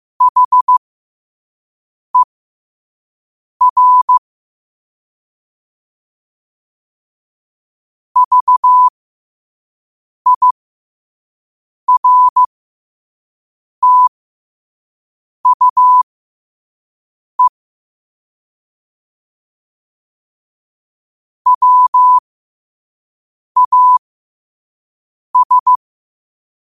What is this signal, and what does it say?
Morse code, heard as a keyed tone, about 15 wpm, spaced as for 5 wpm, 1000 Hz. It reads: HER VIRTUE WAS
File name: qotdfiles / 5wpm